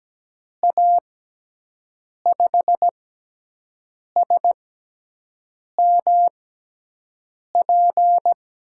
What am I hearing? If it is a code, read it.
Morse code, heard as a keyed tone, about 17 wpm, spaced as for 6 wpm, 700 Hz. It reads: A5SMP
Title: USA Call Signs - 6 Characters - 6x Character Spacing - 17wpm